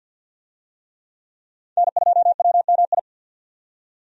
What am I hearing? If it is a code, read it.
Morse code, heard as a keyed tone, about 51 wpm, 700 Hz. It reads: N2WMI
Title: USA Call Signs - 5 Characters - 50wpm